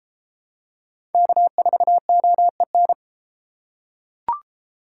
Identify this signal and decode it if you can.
Morse code, heard as a keyed tone, about 30 wpm, 700 Hz. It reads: K4OEN